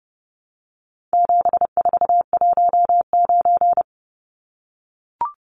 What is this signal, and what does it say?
Morse code, heard as a keyed tone, about 30 wpm, 700 Hz. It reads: 7419
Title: Numbers - 4-Digit 30wpm